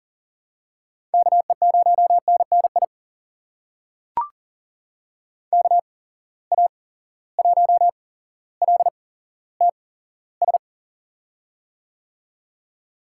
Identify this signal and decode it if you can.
Morse code, heard as a keyed tone, about 40 wpm, spaced as for 17 wpm, 700 Hz. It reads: KE0NNI KA1LTS
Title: USA Call Signs - 6 Characters - 8x Character Spacing - 40wpm